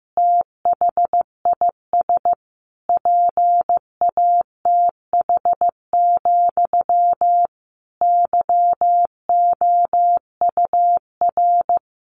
Morse code, heard as a keyed tone, about 15 wpm, 700 Hz. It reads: THIS PATH, YOUR
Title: A Sound Of Thunder 15wpm